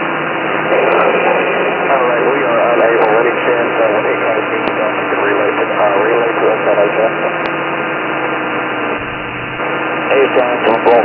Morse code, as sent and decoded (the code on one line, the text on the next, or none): none